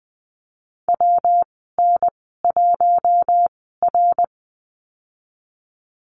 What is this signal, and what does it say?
Morse code, heard as a keyed tone, about 20 wpm, spaced as for 15 wpm, 700 Hz. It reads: WN1R
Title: 193 Lesson 31g - Letter v - Call Signs 20wpm